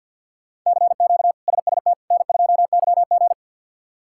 Morse code, heard as a keyed tone, about 47 wpm, 700 Hz. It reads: CQ SST N2YG
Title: K1USN SST QSO Practice - 50wpm